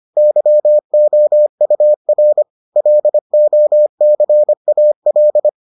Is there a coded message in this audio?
Yes: Morse code from 0.2 to 5.5 s, heard as a keyed tone, about 25 wpm, 600 Hz. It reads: YOUR LOCAL